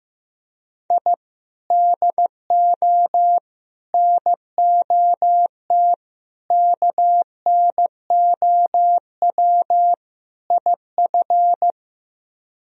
Morse code, heard as a keyed tone, about 15 wpm, 700 Hz. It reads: I DO NOT KNOW IF
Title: Speed-Racing - Sets of 5 Words 15wpm